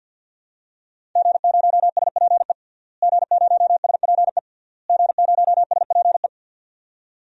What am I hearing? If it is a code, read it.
Morse code, heard as a keyed tone, about 49 wpm, 700 Hz. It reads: G0SPE G0SPE G0SPE
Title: DXCC-100 Call Signs - 5 Characters - Repeated 3x - 50wpm